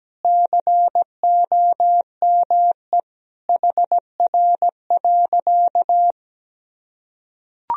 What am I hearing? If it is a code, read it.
Morse code, heard as a keyed tone, about 17 wpm, 700 Hz. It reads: COME HR.
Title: Sentences from Top 300 Words and Abbreviations 17wpm